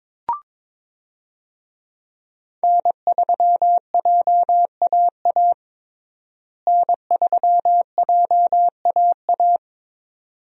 Morse code, heard as a keyed tone, about 22 wpm, 700 Hz. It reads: N3JAA N3JAA
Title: USA Call Signs - 5 Characters - Repeated 2x - 22wpm